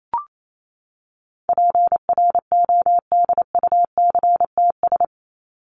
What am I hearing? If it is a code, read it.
Morse code, heard as a keyed tone, about 28 wpm, 700 Hz. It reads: PRODUCTS
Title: ES:S Suffix - Top 100 Words with Base Words 28wpm